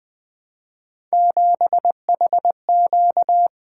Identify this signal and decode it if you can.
Morse code, heard as a keyed tone, about 20 wpm, 700 Hz. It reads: 7HQ
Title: Three Letter-Number Combinations - 20wpm